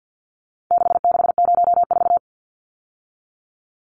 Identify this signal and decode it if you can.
Morse code, heard as a keyed tone, about 50 wpm, 700 Hz. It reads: 6604
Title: Numbers - 4-Digit 50wpm